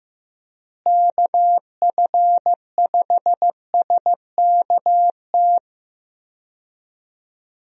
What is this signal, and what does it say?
Morse code, heard as a keyed tone, about 15 wpm, 700 Hz. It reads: KF5SKT
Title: USA Call Signs - 6 Characters - Repeated 2x - 15wpm